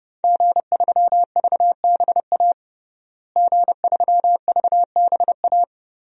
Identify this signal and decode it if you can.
Morse code, heard as a keyed tone, about 30 wpm, 700 Hz. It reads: G3VBA G3VBA
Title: DXCC-100 Call Signs - 5 Characters - Repeated 3x - 30wpm